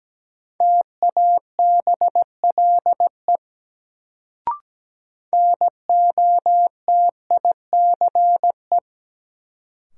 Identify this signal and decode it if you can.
Morse code, heard as a keyed tone, about 17 wpm, 700 Hz. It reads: TABLE NOTICE
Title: Sentences from Top 400 Words - New Word Drill - 17wpm